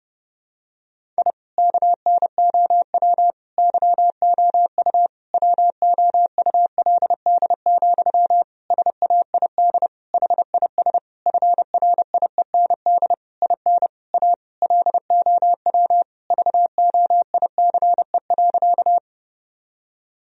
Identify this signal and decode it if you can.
Morse code, heard as a keyed tone, about 30 wpm, 700 Hz. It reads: I KNOW YOU WOULD, SAID HIS FRIEND IN A LOW VOICE.
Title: Sentences from Top 1000 Words - Review Part 1 - 30wpm